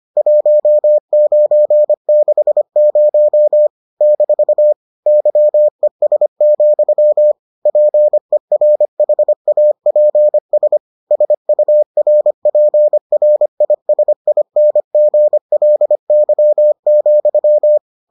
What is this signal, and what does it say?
Morse code, heard as a keyed tone, about 25 wpm, 600 Hz. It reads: 1960 - YES, PERHAPS SURPRISINGLY,